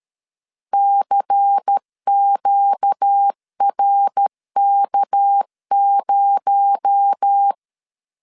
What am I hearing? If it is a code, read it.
Morse code, heard as a keyed tone, about 13 wpm, 800 Hz. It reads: CQRK0
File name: LEZIONI / 6 PKX0 / 6 PKX0 12wpm 2231